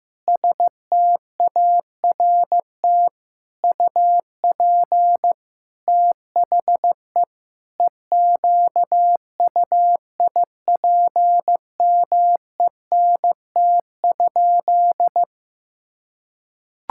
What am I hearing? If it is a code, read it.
Morse code, heard as a keyed tone, about 15 wpm, 700 Hz. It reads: START UP THE EQUIPMENT?
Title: Sentences from Top 1000 Words - Review Part 6 - 15wpm